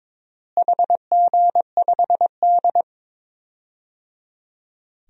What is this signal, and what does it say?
Morse code, heard as a keyed tone, about 22 wpm, 700 Hz. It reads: HG5D
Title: DXCC-100 Call Signs - 4 Characters - Repeated 3x - 22wpm